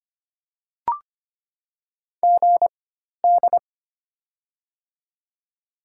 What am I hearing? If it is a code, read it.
Morse code, heard as a keyed tone, about 25 wpm, spaced as for 12 wpm, 700 Hz. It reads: GD
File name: Top 90 QSO Elements - 4x Character Spacing - 25wpm